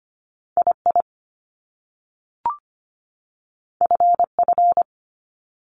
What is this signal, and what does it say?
Morse code, heard as a keyed tone, about 25 wpm, 700 Hz. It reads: II FF
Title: Top Two Letter Combinations in Words 25wpm